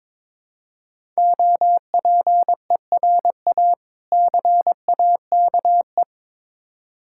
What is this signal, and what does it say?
Morse code, heard as a keyed tone, about 22 wpm, 690 Hz. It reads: OPERA CAKE
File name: Difficult Words with Hints - 22wpm